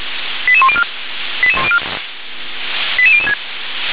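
Five-tone selective call (ZVEI), heard as quick runs of five tones, tones of about 70 ms. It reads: 88104 88006
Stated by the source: Five-tone ZVEI selective call received by SDR.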